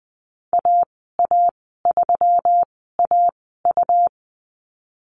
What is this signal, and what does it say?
Morse code, heard as a keyed tone, about 20 wpm, spaced as for 14 wpm, 700 Hz. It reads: AA3AU
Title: 128 Lesson 21g - Number 6 - Call Signs 20wpm